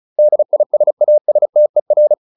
Morse code, heard as a keyed tone, about 34 wpm, 600 Hz. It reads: DISASTER